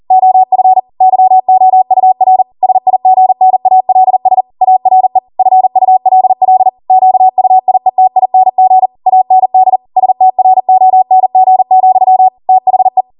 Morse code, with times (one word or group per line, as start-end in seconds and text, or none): 0.1-0.8 OF
1.0-2.4 YOUR
2.6-4.4 SIGNALS
4.6-5.2 ARE
5.4-6.7 FULL
6.9-8.9 QUIETING
9.1-9.8 AND
10.0-12.3 STRONG,
12.5-13.0 THE